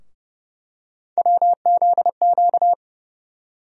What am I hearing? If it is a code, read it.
Morse code, heard as a keyed tone, about 30 wpm, 700 Hz. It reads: WZQ
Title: USA Call Sign Postfixes - 3 Letters 30wpm